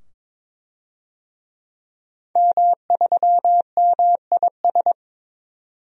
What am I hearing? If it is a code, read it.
Morse code, heard as a keyed tone, about 22 wpm, 700 Hz. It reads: M3MIS
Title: DXCC-100 Call Signs - 5 Characters - 22wpm